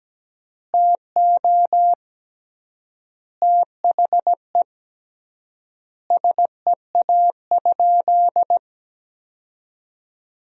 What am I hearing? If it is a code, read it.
Morse code, heard as a keyed tone, about 17 wpm, 700 Hz. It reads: TO THE SEA?
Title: 3x Word-Spacing - Sentences from Top 300 Words - 17wpm